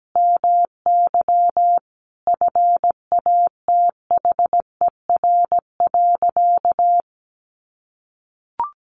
Morse code, heard as a keyed tone, about 17 wpm, 700 Hz. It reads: MY FATHER.
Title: Sentences from Top 1300 Words 17wpm